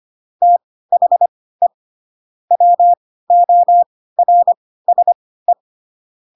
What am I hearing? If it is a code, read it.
Morse code, heard as a keyed tone, about 25 wpm, spaced as for 16 wpm, 700 Hz. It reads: THE WORSE
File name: Pinocchio / Pinocchio ch1 10 25 ik1hge